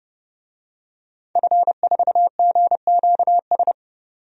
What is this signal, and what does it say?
Morse code, heard as a keyed tone, about 30 wpm, 700 Hz. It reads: F4GQS
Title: DXCC-100 Call Signs - 5 Characters - 30wpm